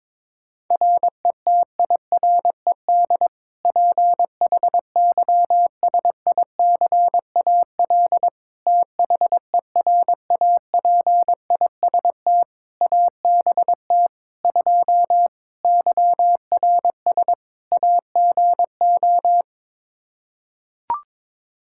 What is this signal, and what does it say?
Morse code, heard as a keyed tone, about 21 wpm, 700 Hz. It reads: RETIRED PHYSICAL THERAPIST ABT 2 YRS AGO
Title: K1USN SST QSO Practice 22wpm